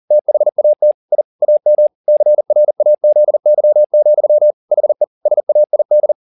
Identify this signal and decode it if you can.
Morse code, heard as a keyed tone, about 40 wpm, 600 Hz. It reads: THAT I AM CRAZY, HE SAID